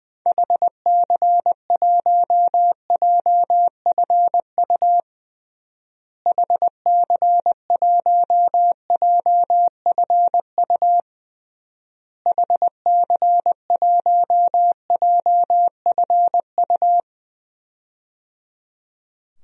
Morse code, heard as a keyed tone, about 21 wpm, 700 Hz. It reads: HC1JFU HC1JFU HC1JFU
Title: DXCC-100 Call Signs - 6 Characters - Repeated 3x - 20wpm